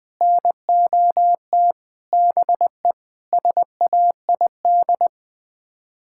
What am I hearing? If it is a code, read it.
Morse code, heard as a keyed tone, about 20 wpm, 700 Hz. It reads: NOT BE SAID
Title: Sets of 6 Words - 20wpm